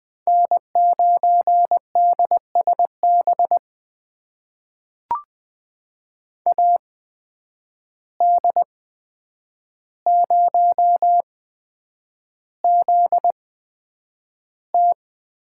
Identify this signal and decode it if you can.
Morse code, heard as a keyed tone, about 20 wpm, spaced as for 9 wpm, 700 Hz. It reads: N9DSB AD0ZT
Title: USA Call Signs - 5 Characters - 8x Character Spacing - 20wpm